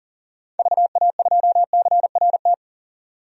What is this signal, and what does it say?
Morse code, heard as a keyed tone, about 40 wpm, 700 Hz. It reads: VA2CRT
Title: DXCC-100 Call Signs - 6 Characters - 40wpm